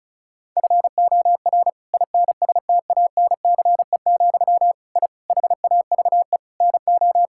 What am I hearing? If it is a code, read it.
Morse code, heard as a keyed tone, about 36 wpm, 700 Hz. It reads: FOR INSTANCE, I HAVE NO
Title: Sentences from Top 1000 Words - Review Part 5 - 35wpm